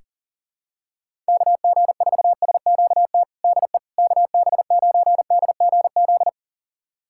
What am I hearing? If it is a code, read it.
Morse code, heard as a keyed tone, about 40 wpm, 700 Hz. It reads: KG4SQT DE KB9DGZ
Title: USA Callsign DE Callsign - 6 Characters - 40wpm